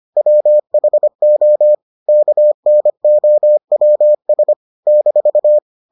Morse code, heard as a keyed tone, about 25 wpm, 600 Hz. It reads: WHO KNOWS -